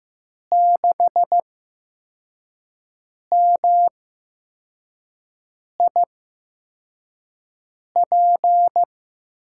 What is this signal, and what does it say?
Morse code, heard as a keyed tone, about 16 wpm, spaced as for 4 wpm, 700 Hz. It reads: 6MIP
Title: USA Call Signs - 6 Characters - 8x Character Spacing - 15wpm